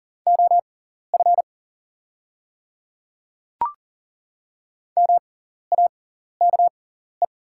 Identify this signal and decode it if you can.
Morse code, heard as a keyed tone, about 40 wpm, spaced as for 14 wpm, 700 Hz. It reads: OF MAKE
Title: 6x Character Spacing - Top 100 Words - 40wpm